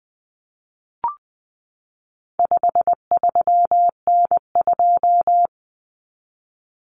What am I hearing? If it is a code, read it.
Morse code, heard as a keyed tone, about 20 wpm, 700 Hz. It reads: 53N2
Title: Common Cut Numbers - 4-Digit 20wpm